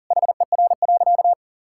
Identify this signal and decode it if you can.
Morse code, heard as a keyed tone, about 40 wpm, 700 Hz. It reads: HER.
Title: Sentences from Top 1000 Words - Review Part 5 - 40wpm